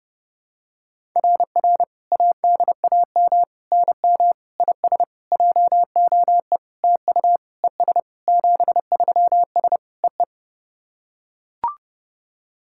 Morse code, heard as a keyed tone, about 30 wpm, 700 Hz. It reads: RR ADAM NM IS JOE TU ES 73S EE
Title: SOTA QSO Practice 30wpm